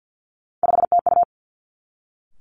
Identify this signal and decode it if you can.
Morse code, heard as a keyed tone, about 50 wpm, 700 Hz. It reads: 5TU
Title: Three Challenging Letter-Number Combinations 50wpm